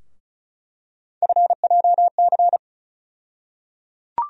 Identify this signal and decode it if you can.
Morse code, heard as a keyed tone, about 34 wpm, 700 Hz. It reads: FJC
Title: USA Call Sign Postfixes - 3 Letters 35wpm